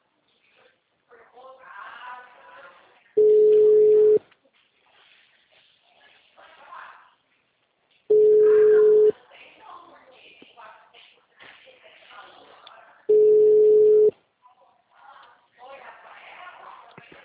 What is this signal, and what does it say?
Ringback tone (Europe, 425 Hz), with speech.